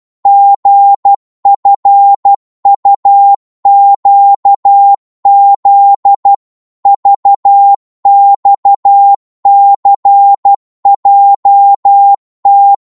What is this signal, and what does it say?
Morse code, heard as a keyed tone, about 12 wpm, 800 Hz. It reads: GFUQZ VXCJT
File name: Morse / ltr-12wpm